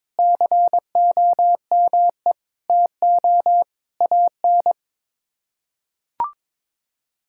Morse code, heard as a keyed tone, about 22 wpm, 700 Hz. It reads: COME TO AN
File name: practice / 22wpm / Sets of 7 Words - 22wpm